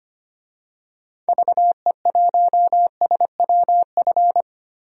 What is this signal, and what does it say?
Morse code, heard as a keyed tone, about 25 wpm, 700 Hz. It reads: VE1SWF
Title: DXCC-100 Call Signs - 6 Characters - Repeated 2x - 25wpm